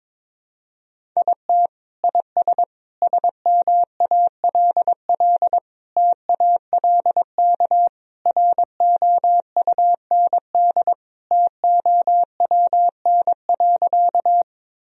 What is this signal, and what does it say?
Morse code, heard as a keyed tone, about 22 wpm, 700 Hz. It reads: IT IS SMALL TALK ROUND TOWN.
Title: Sentences from Top 1000 Words - Review Part 1 - 22wpm